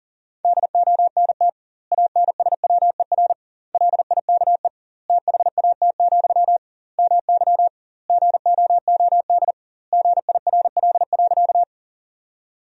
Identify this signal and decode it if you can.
Morse code, heard as a keyed tone, about 40 wpm, 700 Hz. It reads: DONT ANSWER LIKE THAT, MY GOOD GIRL.
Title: 2x Word-Spacing - Sentences from Top 400 Words - 40wpm